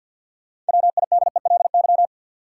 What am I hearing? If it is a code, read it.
Morse code, heard as a keyed tone, about 53 wpm, 700 Hz. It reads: WIDELY
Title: Sentences from Top 1200 Words - New Word Drill Alt - 50wpm